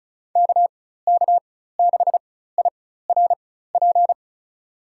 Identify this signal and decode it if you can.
Morse code, heard as a keyed tone, about 34 wpm, spaced as for 16 wpm, 700 Hz. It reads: KK6IRP
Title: USA Call Signs - 6 Characters - 4x Character Spacing - 35wpm